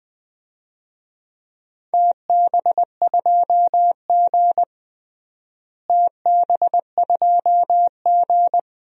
Morse code, heard as a keyed tone, about 20 wpm, 700 Hz. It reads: TB2G TB2G
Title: DXCC-100 Call Signs - 4 Characters - Repeated 3x - 20wpm